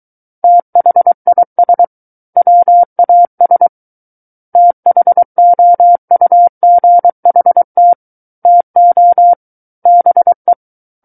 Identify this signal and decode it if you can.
Morse code, heard as a keyed tone, about 23 wpm, 700 Hz. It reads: THIS WAS THOUGHT TO BE
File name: Book One / 14-Chapter 14 - 23 WPM